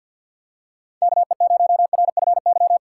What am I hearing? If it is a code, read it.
Morse code, heard as a keyed tone, about 49 wpm, 700 Hz. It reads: KE0RFY
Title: USA Call Signs - 4x Character Spacing - 50wpm